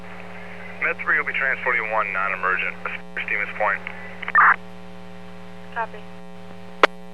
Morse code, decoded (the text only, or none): none